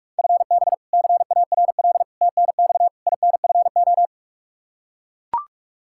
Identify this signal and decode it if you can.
Morse code, heard as a keyed tone, about 44 wpm, 700 Hz. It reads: FB CARL TNX INFO
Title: Practice QSO - Exchange 3 - More Info - 45wpm